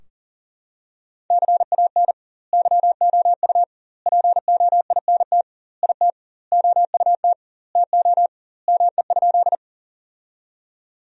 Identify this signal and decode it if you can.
Morse code, heard as a keyed tone, about 40 wpm, 700 Hz. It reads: CAN YOU POINT IT OUT TO ME?
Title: 2x Word-Spacing - Sentences from Top 200 Words - 40wpm